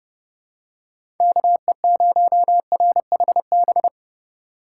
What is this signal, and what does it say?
Morse code, heard as a keyed tone, about 30 wpm, 700 Hz. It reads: KE0RHB
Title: USA Call Signs - 6 Characters - 6x Character Spacing - 30wpm